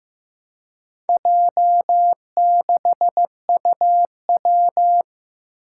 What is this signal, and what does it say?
Morse code, heard as a keyed tone, about 15 wpm, 700 Hz. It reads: J6UW
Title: Four Challenging Letter-Number Combinations 15wpm